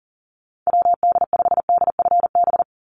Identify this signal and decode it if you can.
Morse code, heard as a keyed tone, about 40 wpm, 700 Hz. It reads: WD5DFB
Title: USA Call Signs 40wpm